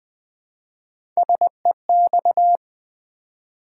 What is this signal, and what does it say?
Morse code, heard as a keyed tone, about 20 wpm, 700 Hz. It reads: SEX
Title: Three Letter Words - 20wpm